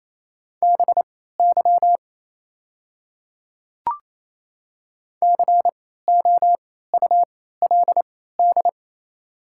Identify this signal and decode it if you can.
Morse code, heard as a keyed tone, about 27 wpm, spaced as for 16 wpm, 700 Hz. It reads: BY COULD
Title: 3x Character Spacing - Top 100 Words - 28wpm